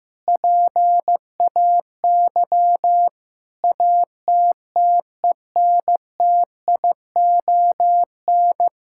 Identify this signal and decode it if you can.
Morse code, heard as a keyed tone, about 15 wpm, 700 Hz. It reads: PAY ATTENTION
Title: Sentences from Top 1000 Words - Review Part 6 - 15wpm